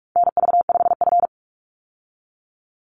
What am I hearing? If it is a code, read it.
Morse code, heard as a keyed tone, about 44 wpm, 700 Hz. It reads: NV5F
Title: Call Signs - 45wpm